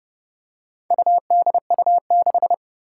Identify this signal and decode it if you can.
Morse code, heard as a keyed tone, about 29 wpm, 700 Hz. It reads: UDU6
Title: Four Challenging Letter-Number Combinations 30wpm